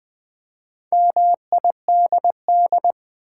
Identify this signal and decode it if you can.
Morse code, heard as a keyed tone, about 20 wpm, 690 Hz. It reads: MIDD
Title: Build A Word - 2 Letters At A Time - 6-Letter Words - 20wpm